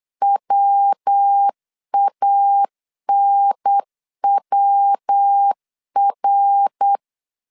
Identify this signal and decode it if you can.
Morse code, heard as a keyed tone, about 8 wpm, 800 Hz. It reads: WANWR
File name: CW / DET / 1 3 1500C S C 8wpm 4214